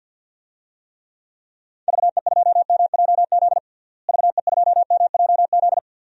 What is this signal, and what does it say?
Morse code, heard as a keyed tone, about 50 wpm, 700 Hz. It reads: VE2MJZ VE2MJZ
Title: DXCC-100 Call Signs - 6 Characters - Repeated 2x - 50wpm